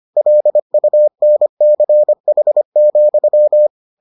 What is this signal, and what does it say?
Morse code, heard as a keyed tone, about 25 wpm, 600 Hz. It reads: LUNCH,